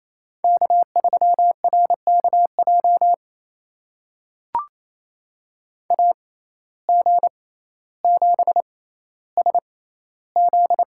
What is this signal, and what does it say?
Morse code, heard as a keyed tone, about 28 wpm, spaced as for 14 wpm, 700 Hz. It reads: K3RKJ AG7SZ
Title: USA Call Signs - 5 Characters - 6x Character Spacing - 28wpm